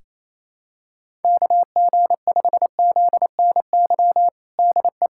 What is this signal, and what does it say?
Morse code, heard as a keyed tone, about 27 wpm, 700 Hz. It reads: KG5ZNY DE
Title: USA Callsign DE Callsign - 6 Characters - 28wpm